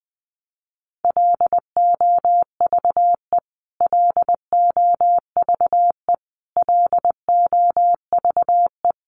Morse code, heard as a keyed tone, about 20 wpm, 700 Hz. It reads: LOVE LOVE LOVE
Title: Top 500 Words Repeated 20wpm